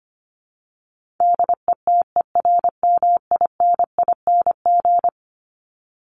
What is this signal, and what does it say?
Morse code, heard as a keyed tone, about 25 wpm, 700 Hz. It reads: DETERMINING
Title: ING Suffix - Top 100 Words with Base Words 25wpm